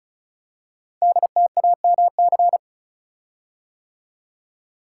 Morse code, heard as a keyed tone, about 36 wpm, 690 Hz. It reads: DTAMC
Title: Code Groups - 5 Letters - 35wpm